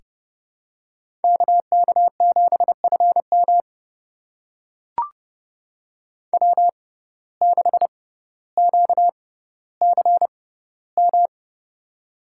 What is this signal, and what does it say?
Morse code, heard as a keyed tone, about 30 wpm, spaced as for 15 wpm, 700 Hz. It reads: KK7FM W6QCM
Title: USA Call Signs - 5 Characters - 6x Character Spacing - 30wpm